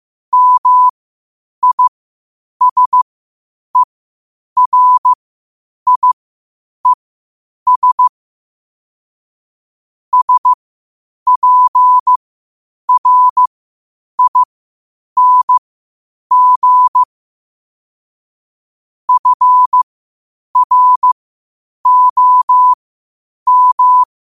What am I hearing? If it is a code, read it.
Morse code, heard as a keyed tone, about 15 wpm, spaced as for 8 wpm, 1000 Hz. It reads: MISERIES SPRING FROM
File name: qotdfiles / 8wpm